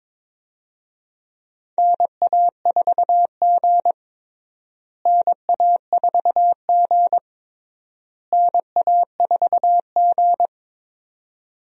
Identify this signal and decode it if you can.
Morse code, heard as a keyed tone, about 21 wpm, 700 Hz. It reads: NA4G NA4G NA4G
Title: USA Call Signs - 4 Characters - Repeated 3x - 22wpm